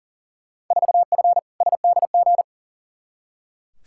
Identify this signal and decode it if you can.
Morse code, heard as a keyed tone, about 40 wpm, 700 Hz. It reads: 4F SDG
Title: Field Day Exchanges - 40wpm